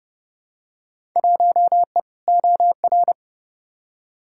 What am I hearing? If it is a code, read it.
Morse code, heard as a keyed tone, about 30 wpm, 700 Hz. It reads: 1E OR
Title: Field Day Exchanges Encore - 30wpm